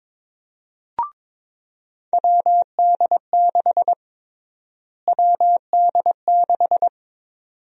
Morse code, heard as a keyed tone, about 22 wpm, 700 Hz. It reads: WD6 WD6
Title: USA Call Sign Prefixes - Repeated 2x - 22wpm